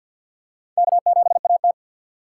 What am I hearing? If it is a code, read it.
Morse code, heard as a keyed tone, about 52 wpm, 700 Hz. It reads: K7AT
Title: USA Call Signs - 4 Characters - 6x Character Spacing - 50wpm